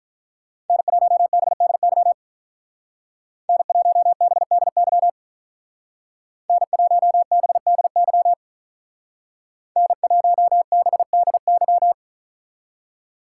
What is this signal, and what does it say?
Morse code, heard as a keyed tone, about 43 wpm, 700 Hz. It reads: N1BDY N1BDY N1BDY N1BDY
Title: Speed Racing - USA Call Signs 35wpm